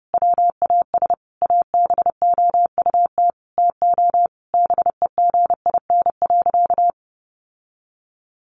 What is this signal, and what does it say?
Morse code, heard as a keyed tone, about 30 wpm, 700 Hz. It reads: WAS ABOUT TO BEGIN.
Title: Sentences from Top 1200 Words 30wpm